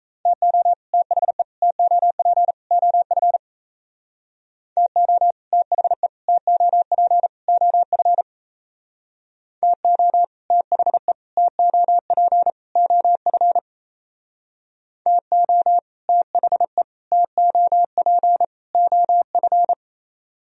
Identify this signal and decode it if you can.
Morse code, heard as a keyed tone, about 35 wpm, 700 Hz. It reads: TO THE TOP OF TO THE TOP OF TO THE TOP OF TO THE TOP OF
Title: Speed-Racing - Sets of 4 Words 28wpm